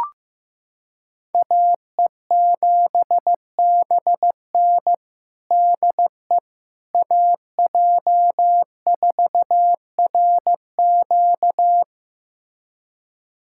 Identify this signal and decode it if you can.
Morse code, heard as a keyed tone, about 15 wpm, 700 Hz. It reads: AE7BN DE AJ4RQ
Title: USA Callsign DE Callsign - 5 Characters - 15wpm